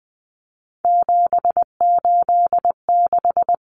Morse code, heard as a keyed tone, about 20 wpm, 700 Hz. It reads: 786
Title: Common Cut Numbers - 3-Digit 20wpm